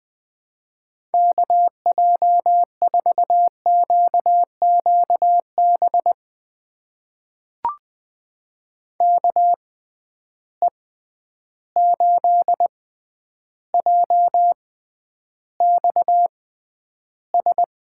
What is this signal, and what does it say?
Morse code, heard as a keyed tone, about 20 wpm, spaced as for 10 wpm, 700 Hz. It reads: KJ4QQB KE8JXS
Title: USA Call Signs - 6x Character Spacing - 20wpm